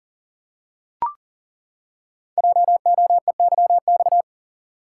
Morse code, heard as a keyed tone, about 39 wpm, 700 Hz. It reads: JOEYX
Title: Code Groups - 5 Letter:Numbers - 40wpm